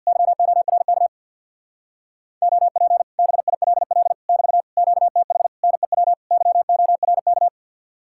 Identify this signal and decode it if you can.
Morse code, heard as a keyed tone, about 64 wpm, 700 Hz. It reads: YORK OP BILL <BT> QTH NEW YORK